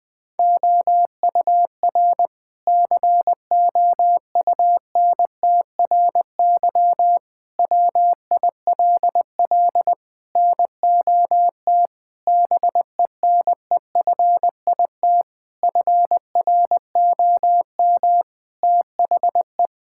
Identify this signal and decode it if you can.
Morse code, heard as a keyed tone, about 20 wpm, 700 Hz. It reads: OUR COUNTRY WILL NOT BENEFIT FROM THE
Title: Sentences from Top 1000 Words - Review Part 6 - 20wpm